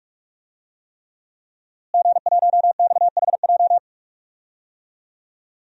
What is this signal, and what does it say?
Morse code, heard as a keyed tone, about 46 wpm, 700 Hz. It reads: G1XHJ
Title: DXCC-100 Call Signs - 5 Characters - Repeated 3x - 45wpm